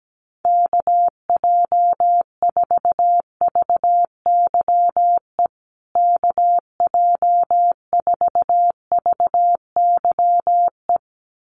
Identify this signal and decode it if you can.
Morse code, heard as a keyed tone, about 17 wpm, 700 Hz. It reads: KJ4VYE KJ4VYE
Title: USA Call Signs Repeated 2x 17wpm